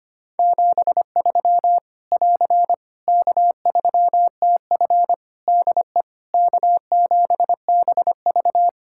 Morse code, heard as a keyed tone, about 25 wpm, 700 Hz. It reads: 73 <AR> K3TF DE K7BV
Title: Practice QSO - Exchange 4 - Ending - 25wpm